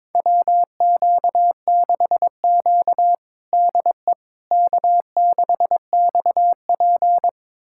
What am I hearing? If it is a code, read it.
Morse code, heard as a keyed tone, about 21 wpm, 700 Hz. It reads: WQ6Q DE K6XP
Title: USA Callsign DE Callsign - 4 Characters - 22wpm